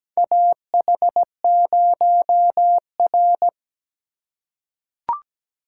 Morse code, heard as a keyed tone, about 17 wpm, 700 Hz. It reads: AH0R
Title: USA Call Signs - 4 Characters - 4x Character Spacing - 17wpm